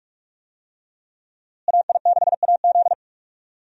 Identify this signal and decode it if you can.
Morse code, heard as a keyed tone, about 46 wpm, 700 Hz. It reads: AI6AZ
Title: USA Call Signs - 5 Characters - Repeated 2x - 45wpm